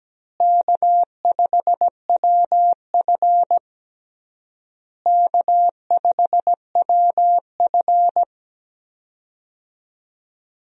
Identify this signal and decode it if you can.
Morse code, heard as a keyed tone, about 17 wpm, 700 Hz. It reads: K5WF K5WF
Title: USA Call Signs - 4 Characters - Repeated 3x - 17wpm